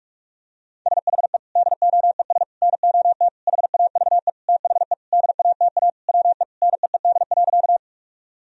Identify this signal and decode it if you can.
Morse code, heard as a keyed tone, about 46 wpm, 700 Hz. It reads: SHE DOES NOT HAVE THE DATA WE NEED.
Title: Sentences from Top 1000 Words - Review Part 3 - 45wpm